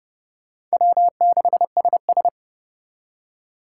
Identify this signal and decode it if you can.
Morse code, heard as a keyed tone, about 39 wpm, 700 Hz. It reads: W6SS